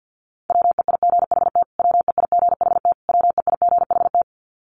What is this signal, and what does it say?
Morse code, heard as a keyed tone, about 50 wpm, 700 Hz. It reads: WEIGHT WEIGHT WEIGHT
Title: Top 500 Words Repeated 50wpm